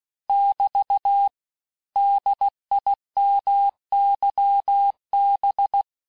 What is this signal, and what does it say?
Morse code, heard as a keyed tone, about 16 wpm, 800 Hz. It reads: <BT> DIMYB
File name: cours-du-lundi files / Cours-1586-texte-chiffre-vitesse-15mpm